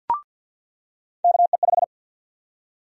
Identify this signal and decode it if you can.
Morse code, heard as a keyed tone, about 49 wpm, 700 Hz. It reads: KE5
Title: USA Call Sign Prefixes - 50wpm